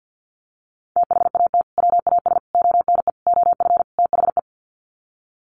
Morse code, heard as a keyed tone, about 50 wpm, 700 Hz. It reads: THAT WAS ONE OF THE
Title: Sets of 5 Words 50wpm